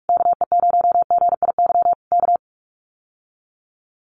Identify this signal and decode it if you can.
Morse code, heard as a keyed tone, about 44 wpm, 700 Hz. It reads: KE0GIY K
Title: Common QSO Phrases Encore 45wpm